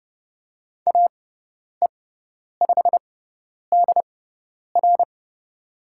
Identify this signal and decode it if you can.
Morse code, heard as a keyed tone, about 30 wpm, spaced as for 10 wpm, 700 Hz. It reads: AE5DR
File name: Lesson 17g - Letter u - Call Signs 30wpm